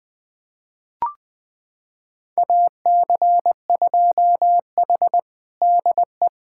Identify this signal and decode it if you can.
Morse code, heard as a keyed tone, about 20 wpm, 700 Hz. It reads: AC2H DE
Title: USA Callsign DE Callsign - 4 Characters - 20wpm